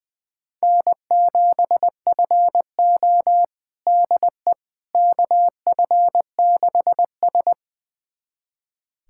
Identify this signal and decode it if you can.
Morse code, heard as a keyed tone, about 21 wpm, 700 Hz. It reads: N7FO DE KF6S
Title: USA Callsign DE Callsign - 4 Characters - 20wpm